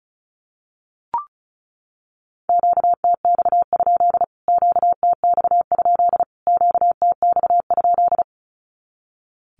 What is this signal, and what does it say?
Morse code, heard as a keyed tone, about 34 wpm, 700 Hz. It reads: QTX? QTX? QTX?
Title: Q-Codes Repeated 3x 35wpm